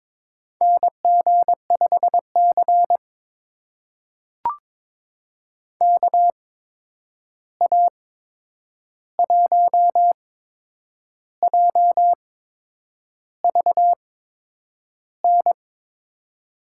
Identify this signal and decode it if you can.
Morse code, heard as a keyed tone, about 21 wpm, spaced as for 8 wpm, 700 Hz. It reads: NG5C KA1JVN
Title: USA Call Signs - 8x Character Spacing - 22wpm